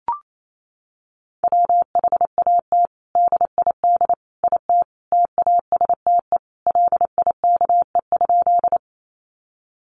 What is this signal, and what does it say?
Morse code, heard as a keyed tone, about 28 wpm, 700 Hz. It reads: WHAT DID IT TASTE LIKE?
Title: Sentences from Top 1100 Words 28wpm